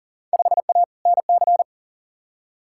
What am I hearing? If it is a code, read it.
Morse code, heard as a keyed tone, about 40 wpm, 700 Hz. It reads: 5A NC
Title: Field Day Exchanges - 40wpm